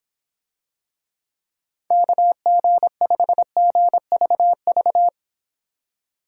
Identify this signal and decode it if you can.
Morse code, heard as a keyed tone, about 26 wpm, 700 Hz. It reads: KG5GVV